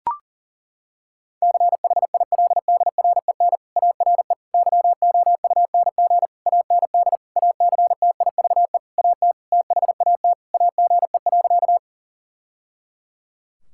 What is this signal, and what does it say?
Morse code, heard as a keyed tone, about 40 wpm, 700 Hz. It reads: CHILDREN ARE YOUNG AND ACTIVE AT THAT AGE.
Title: Sentences from Top 1000 Words - Review Part 6 - 40wpm